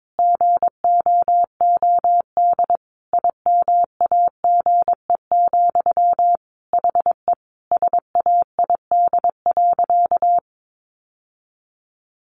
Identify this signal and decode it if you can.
Morse code, heard as a keyed tone, about 22 wpm, 700 Hz. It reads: GOOD IMAGE, HE SAID.
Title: Sentences from Top 700 Words 22wpm